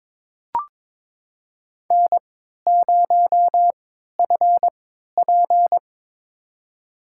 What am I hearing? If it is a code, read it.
Morse code, heard as a keyed tone, about 22 wpm, spaced as for 13 wpm, 700 Hz. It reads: N0FP
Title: USA Call Signs - 4 Characters - 3x Character Spacing - 22wpm